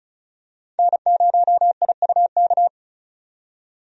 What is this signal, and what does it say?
Morse code, heard as a keyed tone, about 36 wpm, 700 Hz. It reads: N0IUK
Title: USA Call Signs - 5 Characters - 35wpm